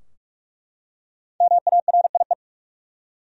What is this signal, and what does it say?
Morse code, heard as a keyed tone, about 43 wpm, 700 Hz. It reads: MARIE
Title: USA Common Names - Max 5 Letters - 45wpm